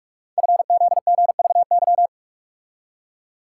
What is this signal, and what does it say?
Morse code, heard as a keyed tone, about 44 wpm, 700 Hz. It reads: FZGVY
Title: Code Groups - 5 Letters - 45wpm